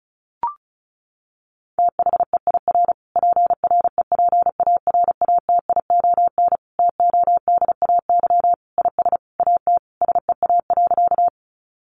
Morse code, heard as a keyed tone, about 34 wpm, 700 Hz. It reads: THEIR PREPARATION TODAY IS AT SEA.
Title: Sentences from Top 1400 Words 35wpm